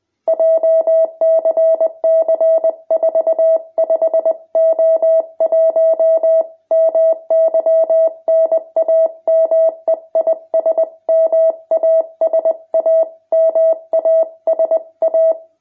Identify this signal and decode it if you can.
Morse code, heard as a keyed tone, about 20 wpm, 650 Hz. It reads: JCC 45O1 MY NAME IS MASA MASA
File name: jh8cbh / 2021cw